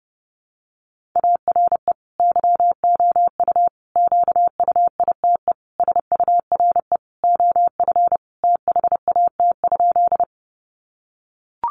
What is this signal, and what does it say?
Morse code, heard as a keyed tone, about 30 wpm, 700 Hz. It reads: ARE YOU QUITE SURE OF THAT?
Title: Sentences from Top 700 Words 30wpm